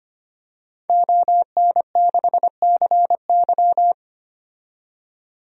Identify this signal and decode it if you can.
Morse code, heard as a keyed tone, about 25 wpm, 700 Hz. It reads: ON6CY
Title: DXCC-100 Call Signs - 5 Characters - 25wpm